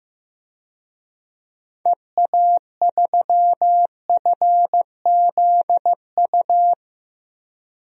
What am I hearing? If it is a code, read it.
Morse code, heard as a keyed tone, about 15 wpm, 690 Hz. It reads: EA3FZU
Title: DXCC-100 Call Signs - 6 Characters - 15wpm